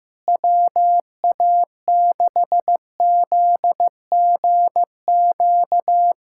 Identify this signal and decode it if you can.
Morse code, heard as a keyed tone, about 16 wpm, 700 Hz. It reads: WA6ZGQ
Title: USA Callsign DE Callsign - 6 Characters - 15wpm